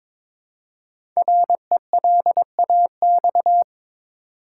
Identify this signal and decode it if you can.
Morse code, heard as a keyed tone, about 21 wpm, 700 Hz. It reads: RELAX
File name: 5 Letter Words - First 3 Letters Given - 22wpm